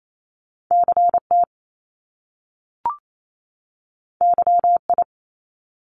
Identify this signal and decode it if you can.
Morse code, heard as a keyed tone, about 28 wpm, 700 Hz. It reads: CT YI
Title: Top Two Letter Combinations in Words 28wpm